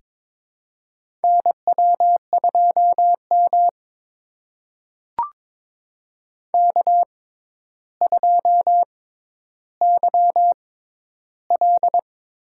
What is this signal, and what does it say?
Morse code, heard as a keyed tone, about 23 wpm, spaced as for 11 wpm, 700 Hz. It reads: NW2M K2YL
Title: USA Call Signs - 4 Characters - 6x Character Spacing - 22wpm